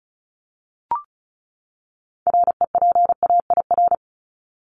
Morse code, heard as a keyed tone, about 34 wpm, 700 Hz. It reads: REPAIR
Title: Top 1501-1750 Words 35wpm